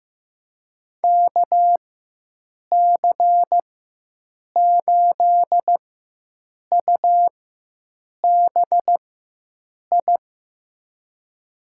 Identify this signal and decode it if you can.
Morse code, heard as a keyed tone, about 15 wpm, spaced as for 7 wpm, 700 Hz. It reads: KC8UBI
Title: USA Call Signs - 6 Characters - 4x Character Spacing - 15wpm